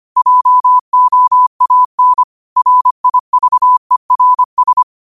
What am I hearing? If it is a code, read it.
Morse code, heard as a keyed tone, about 25 wpm, 1000 Hz. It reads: JOAN RIVERS
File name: qotdfiles / 25wpm